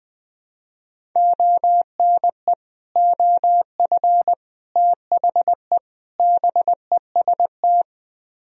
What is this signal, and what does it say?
Morse code, heard as a keyed tone, about 20 wpm, 700 Hz. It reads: ONE OF THE BEST
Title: Speed-Racing - Sets of 4 Words 17wpm